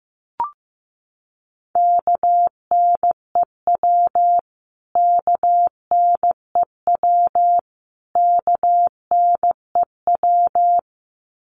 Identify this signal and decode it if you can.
Morse code, heard as a keyed tone, about 15 wpm, 700 Hz. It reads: KNEW KNEW KNEW
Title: Top 500 Words Repeated 15wpm